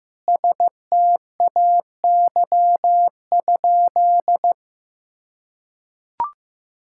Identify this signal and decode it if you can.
Morse code, heard as a keyed tone, about 15 wpm, 700 Hz. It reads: STAY?
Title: Sentences from Top 1600 Words 15wpm